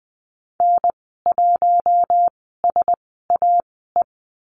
Morse code, heard as a keyed tone, about 20 wpm, spaced as for 15 wpm, 700 Hz. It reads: N1SAE
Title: 055 Lesson 10g - Letter r - Call Signs 20wpm